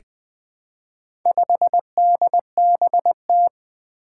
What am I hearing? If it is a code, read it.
Morse code, heard as a keyed tone, about 20 wpm, 700 Hz. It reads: 5DBT
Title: Four Challenging Letter-Number Combinations 20wpm